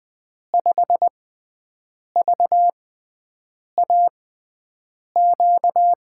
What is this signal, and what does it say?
Morse code, heard as a keyed tone, about 20 wpm, spaced as for 7 wpm, 700 Hz. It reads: 5VAQ
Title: USA Call Signs - 6 Characters - 6x Character Spacing - 20wpm